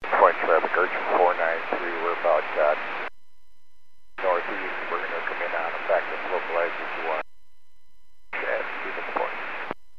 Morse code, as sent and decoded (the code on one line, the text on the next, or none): none